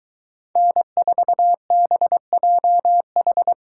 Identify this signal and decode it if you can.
Morse code, heard as a keyed tone, about 23 wpm, 700 Hz. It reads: N4BJH